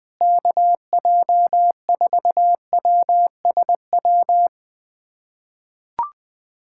Morse code, heard as a keyed tone, about 20 wpm, 700 Hz. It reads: KJ4WSW
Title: USA Callsign DE Callsign - 6 Characters - 20wpm